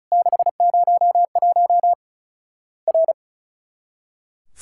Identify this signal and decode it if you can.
Morse code, heard as a keyed tone, about 36 wpm, 700 Hz. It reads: 601 R
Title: Worked All Europe DX Contest - QTC Exchange - 35wpm